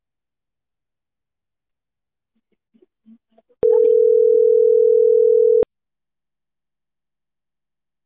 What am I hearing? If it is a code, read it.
Ringback tone (North America), 440 + 480 Hz.